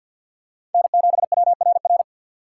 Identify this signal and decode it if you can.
Morse code, heard as a keyed tone, about 49 wpm, 700 Hz. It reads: N7WRR
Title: USA Call Signs - 5 Characters - 4x Character Spacing - 50wpm